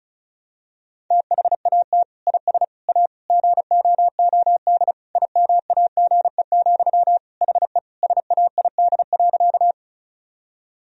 Morse code, heard as a keyed tone, about 36 wpm, 700 Hz. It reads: THAT IS A GOOD IMAGE, HE SAID.
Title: Sentences from Top 1000 Words - Review Part 4 - 35wpm